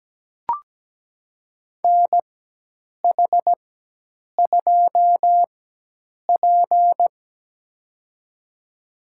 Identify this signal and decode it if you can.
Morse code, heard as a keyed tone, about 17 wpm, spaced as for 8 wpm, 700 Hz. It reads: NH2P
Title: USA Call Signs - 4 Characters - 4x Character Spacing - 17wpm